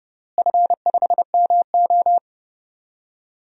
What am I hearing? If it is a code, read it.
Morse code, heard as a keyed tone, about 30 wpm, 700 Hz. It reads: F5MO
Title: DXCC-100 Call Signs - 4 Characters - Repeated 2x - 30wpm